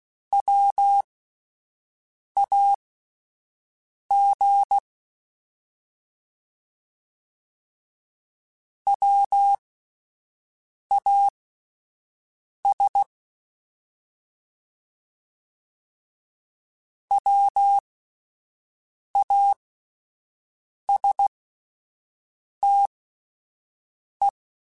Morse code, heard as a keyed tone, about 16 wpm, spaced as for 5 wpm, 800 Hz. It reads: WAG WAS WASTE
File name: Alphabetical / Alphabetical mp3s / Ws @5